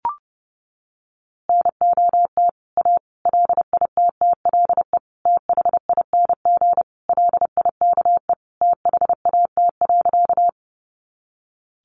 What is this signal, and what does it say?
Morse code, heard as a keyed tone, about 30 wpm, 700 Hz. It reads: NOT A LITTLE THING LIKE THAT.
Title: A Sound Of Thunder 30wpm